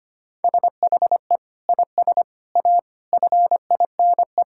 Morse code, heard as a keyed tone, about 25 wpm, 700 Hz. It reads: SHE IS A FINE
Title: Sentences from Top 1000 Words - Review Part 6 - 25wpm